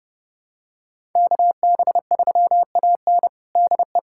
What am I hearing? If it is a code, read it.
Morse code, heard as a keyed tone, about 30 wpm, 700 Hz. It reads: KB3AN DE
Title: USA Callsign DE Callsign - 5 Characters - 30wpm